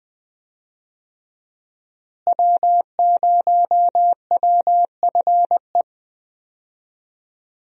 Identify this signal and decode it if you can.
Morse code, heard as a keyed tone, about 20 wpm, 700 Hz. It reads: W0WFE
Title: USA Call Signs - 5 Characters - Repeated 3x - 20wpm